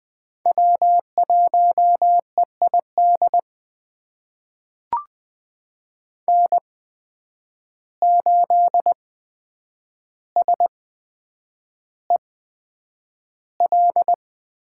Morse code, heard as a keyed tone, about 20 wpm, spaced as for 9 wpm, 700 Hz. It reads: W1EID N8SEL
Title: USA Call Signs - 8x Character Spacing - 20wpm